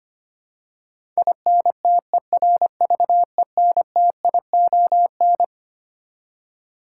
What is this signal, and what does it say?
Morse code, heard as a keyed tone, about 25 wpm, 700 Hz. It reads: INTERVENTION
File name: Twelve Letter Words - 25wpm